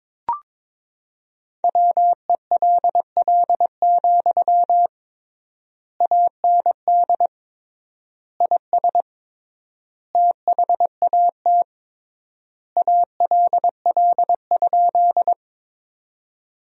Morse code, heard as a keyed tone, about 22 wpm, 700 Hz. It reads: WELL, AND IS THAT ALL?
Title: 3x Word-Spacing - Sentences from Top 200 Words - 22wpm